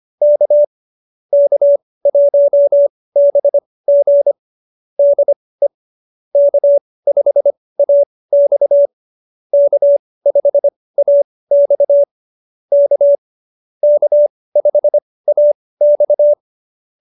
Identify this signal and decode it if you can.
Morse code, heard as a keyed tone, about 25 wpm, spaced as for 18 wpm, 590 Hz. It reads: K K1BG DE K5AX K5AX K K5AX
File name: qso201b 18